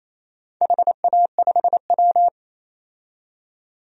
Morse code, heard as a keyed tone, about 28 wpm, 700 Hz. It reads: HA5W
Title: DXCC-100 Call Signs - 4 Characters - 28wpm